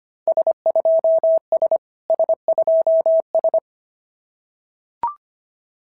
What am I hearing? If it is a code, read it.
Morse code, heard as a keyed tone, about 25 wpm, 690 Hz. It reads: S2S S2S
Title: SOTA QSO Practice 25wpm